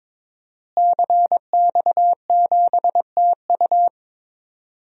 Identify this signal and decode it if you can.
Morse code, heard as a keyed tone, about 22 wpm, 700 Hz. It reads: CX7TU
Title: DXCC-100 Call Signs - 5 Characters - Repeated 3x - 22wpm